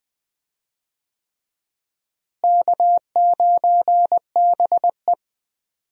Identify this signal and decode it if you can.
Morse code, heard as a keyed tone, about 21 wpm, 700 Hz. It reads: K9BE
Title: USA Call Signs - 4 Characters - Repeated 3x - 20wpm